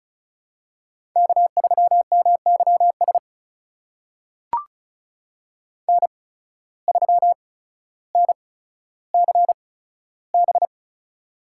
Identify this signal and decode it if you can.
Morse code, heard as a keyed tone, about 36 wpm, spaced as for 15 wpm, 700 Hz. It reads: K3MYS N3NCB
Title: USA Call Signs - 5 Characters - 8x Character Spacing - 35wpm